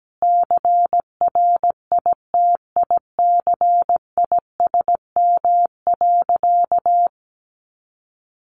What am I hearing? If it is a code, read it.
Morse code, heard as a keyed tone, about 17 wpm, 700 Hz. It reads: CRITICISM.
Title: Sentences from Top 1300 Words 17wpm